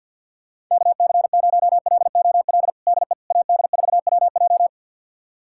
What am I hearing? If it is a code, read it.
Morse code, heard as a keyed tone, about 48 wpm, 690 Hz. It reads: KC0LOL DE AD4WJ
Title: USA Callsign DE Callsign 50wpm